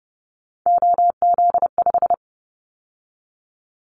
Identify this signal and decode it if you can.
Morse code, heard as a keyed tone, about 30 wpm, 700 Hz. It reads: OZ5
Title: Callsign Prefixes for Top 50 DXCC - 30wpm